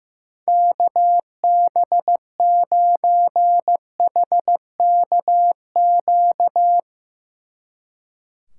Morse code, heard as a keyed tone, about 15 wpm, 700 Hz. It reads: KB9HKQ
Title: Speed Racing - USA 2x3 Call Signs 15wpm